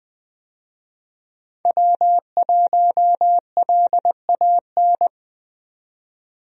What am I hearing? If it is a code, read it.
Morse code, heard as a keyed tone, about 20 wpm, 700 Hz. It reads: W1LAN
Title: USA Call Signs - 5 Characters - 20wpm